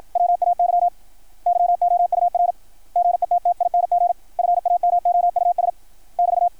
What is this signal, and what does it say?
Morse code, heard as a keyed tone, about 53 wpm, 660 Hz. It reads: WAY YOUR GETTING FAMOUS <BT>
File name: archived / CRSnet